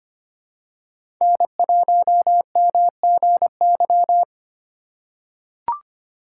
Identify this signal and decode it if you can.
Morse code, heard as a keyed tone, about 25 wpm, 700 Hz. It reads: N1MGY
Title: USA Call Signs - 5 Characters - 4x Character Spacing - 25wpm